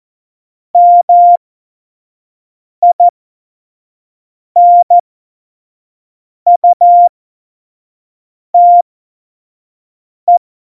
Morse code, heard as a keyed tone, about 14 wpm, spaced as for 5 wpm, 700 Hz. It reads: MINUTE
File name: Morsekurs / Witze / Witz 28. November 2017, 25 F70